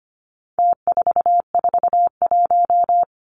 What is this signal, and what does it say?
Morse code, heard as a keyed tone, about 25 wpm, 700 Hz. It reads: T441
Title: Numbers - Contest Style - 4-Digit 25wpm